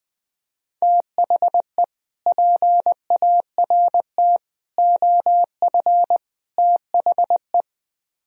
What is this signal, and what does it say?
Morse code, heard as a keyed tone, about 20 wpm, 700 Hz. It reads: THE PART OF THE